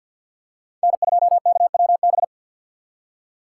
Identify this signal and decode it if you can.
Morse code, heard as a keyed tone, about 53 wpm, 700 Hz. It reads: N2KWB
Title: USA Call Signs - 5 Characters - 6x Character Spacing - 50wpm